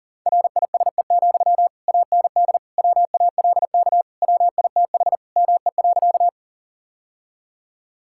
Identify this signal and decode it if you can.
Morse code, heard as a keyed tone, about 40 wpm, 700 Hz. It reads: RISE, AND WALK WITH ME.
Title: Sentences from Top 1000 Words - Review Part 6 - 40wpm